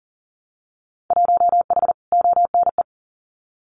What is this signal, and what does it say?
Morse code, heard as a keyed tone, about 41 wpm, 700 Hz. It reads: 1H ONE
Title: Winter Field Day Exchanges 40wpm